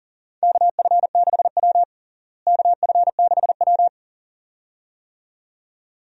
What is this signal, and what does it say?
Morse code, heard as a keyed tone, about 37 wpm, 700 Hz. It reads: KF6W KF6W
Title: USA Call Signs - 4 Characters - Repeated 2x - 40wpm